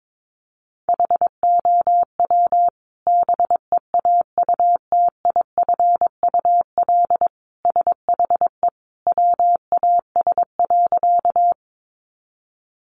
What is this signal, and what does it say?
Morse code, heard as a keyed tone, about 22 wpm, 700 Hz. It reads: HOW BEAUTIFUL SHE WAS.
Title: Sentences from Top 900 Words 22wpm